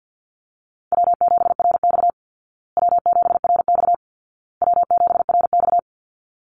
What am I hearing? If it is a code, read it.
Morse code, heard as a keyed tone, about 50 wpm, 700 Hz. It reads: W7RX W7RX W7RX
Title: Call Signs Repeated 2x - 50wpm